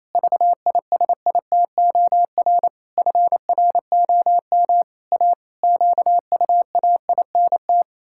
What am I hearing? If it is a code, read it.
Morse code, heard as a keyed tone, about 28 wpm, 700 Hz. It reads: VISITOR FROM A QUAINT
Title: Pangram Challenge v2 - 28wpm